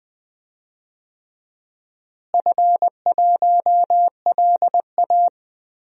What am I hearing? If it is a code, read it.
Morse code, heard as a keyed tone, about 20 wpm, 700 Hz. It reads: F1LA
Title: DXCC-100 Call Signs - 4 Characters - Repeated 3x - 20wpm